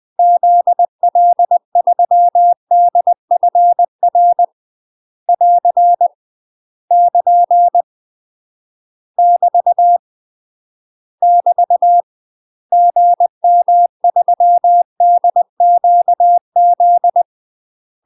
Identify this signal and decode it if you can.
Morse code, heard as a keyed tone, about 20 wpm, 700 Hz. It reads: ZL3DFR <AR> <KN> <BT> <BT> GM3DQZ
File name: UK MASTER QSOs TRAINING / 20WPM / UK MASTER QSOs 0028